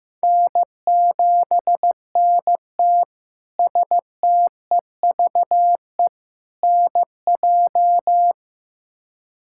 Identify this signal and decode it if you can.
Morse code, heard as a keyed tone, about 15 wpm, 700 Hz. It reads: N7NT STEVE NJ
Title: K1USN SST QSO Practice - 15wpm